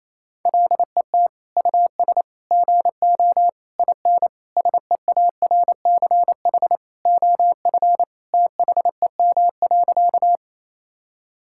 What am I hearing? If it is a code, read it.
Morse code, heard as a keyed tone, about 27 wpm, 700 Hz. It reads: LET US GO IN SEARCH OF THEM.
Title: Sentences from Top 1000 Words - Review Part 5 - 28wpm